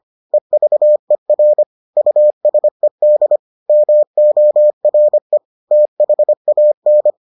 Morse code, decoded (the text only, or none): EVER USED MORE THAN